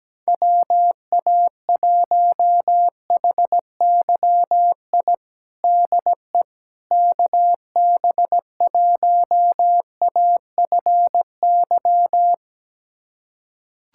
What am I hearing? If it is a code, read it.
Morse code, heard as a keyed tone, about 17 wpm, 700 Hz. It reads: WA1HYI DE KB1AFY
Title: USA Callsign DE Callsign - 6 Characters - 17wpm